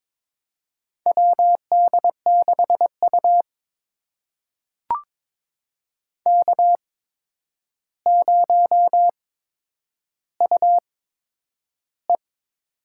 Morse code, heard as a keyed tone, about 22 wpm, spaced as for 9 wpm, 700 Hz. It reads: WD6U K0UE
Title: USA Call Signs - 4 Characters - 8x Character Spacing - 22wpm